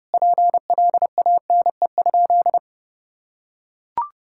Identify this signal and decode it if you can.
Morse code, heard as a keyed tone, about 30 wpm, 700 Hz. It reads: PLANE?
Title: Sentences from Top 1000 Words - Review Part 6 - 30wpm